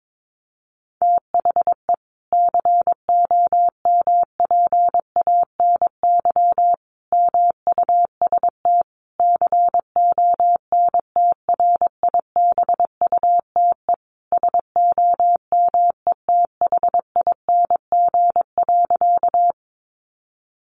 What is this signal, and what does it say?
Morse code, heard as a keyed tone, about 22 wpm, 700 Hz. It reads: THE COMPANY MUST CONTRIBUTE SOMETHING.
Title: Sentences from Top 1400 Words 22wpm